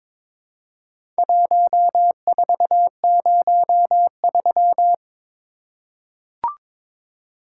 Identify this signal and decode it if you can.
Morse code, heard as a keyed tone, about 23 wpm, 700 Hz. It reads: 1403
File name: practice / 22wpm / Four Letter-Number Combinations - 22wpm